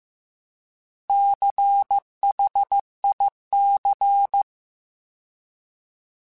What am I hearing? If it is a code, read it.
Morse code, heard as a keyed tone, about 15 wpm, 800 Hz. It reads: CHIC